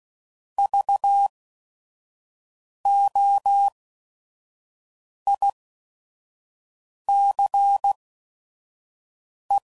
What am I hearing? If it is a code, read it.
Morse code, heard as a keyed tone, about 16 wpm, spaced as for 5 wpm, 800 Hz. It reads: VOICE
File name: Prefixes / By First Letter / Vs